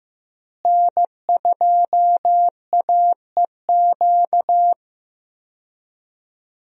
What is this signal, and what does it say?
Morse code, heard as a keyed tone, about 15 wpm, 700 Hz. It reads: N2AEQ
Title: USA Call Signs - 5 Characters - Repeated 3x - 15wpm